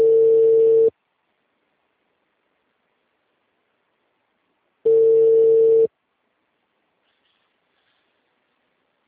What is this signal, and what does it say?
Ringback tone (China, 450 Hz).